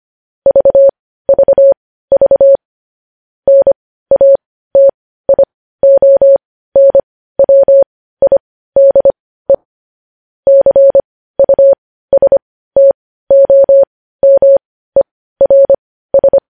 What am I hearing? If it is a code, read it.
Morse code, heard as a keyed tone, about 25 wpm, spaced as for 15 wpm, 550 Hz. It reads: VVV NATIONWIDE CUSTOMERS
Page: news headlines in morse code at 15 wpm